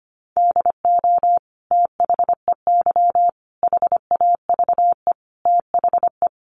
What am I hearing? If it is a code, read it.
Morse code, heard as a keyed tone, about 25 wpm, 700 Hz. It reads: DO THEY HAVE THE
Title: Sentences from Top 1300 Words 25wpm